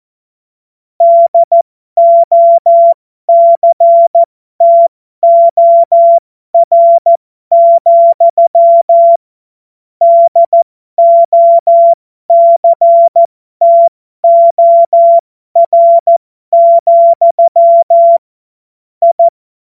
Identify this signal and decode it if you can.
Morse code, heard as a keyed tone, about 14 wpm, spaced as for 12 wpm, 680 Hz. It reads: DOCTOR, DOCTOR, I
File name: Morsekurs / Witze